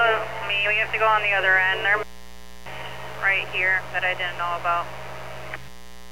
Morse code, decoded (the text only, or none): none